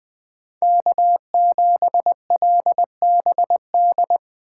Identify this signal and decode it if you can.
Morse code, heard as a keyed tone, about 20 wpm, 700 Hz. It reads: K7LBD
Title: Speed Racing - USA 1x3 Call Signs 17wpm